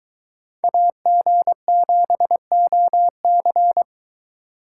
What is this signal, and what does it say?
Morse code, heard as a keyed tone, about 24 wpm, 700 Hz. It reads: AG7OC